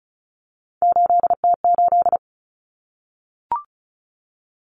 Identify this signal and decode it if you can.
Morse code, heard as a keyed tone, about 34 wpm, 700 Hz. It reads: 8T8
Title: Common Cut Numbers - 3-Digit 35wpm